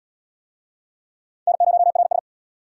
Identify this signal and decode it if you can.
Morse code, heard as a keyed tone, about 68 wpm, 700 Hz. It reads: N0RS